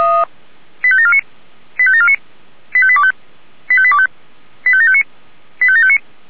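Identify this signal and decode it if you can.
Five-tone selective call (ZVEI), heard as quick runs of five tones, tones of about 70 ms. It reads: 85749 85749 85825 85825 85759 85759
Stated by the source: ZVEI five-tone selective call received on SDR.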